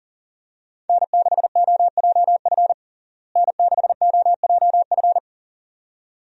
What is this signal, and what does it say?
Morse code, heard as a keyed tone, about 37 wpm, 700 Hz. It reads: N6OJF N6OJF
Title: USA Call Signs - 5 Characters - Repeated 2x - 40wpm